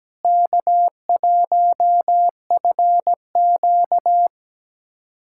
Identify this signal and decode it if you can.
Morse code, heard as a keyed tone, about 17 wpm, 700 Hz. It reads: K1FQ
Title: USA Callsign DE Callsign - 4 Characters - 17wpm